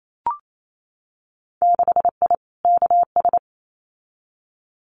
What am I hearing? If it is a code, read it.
Morse code, heard as a keyed tone, about 28 wpm, 700 Hz. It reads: 6I KS
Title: Winter Field Day Exchanges 28wpm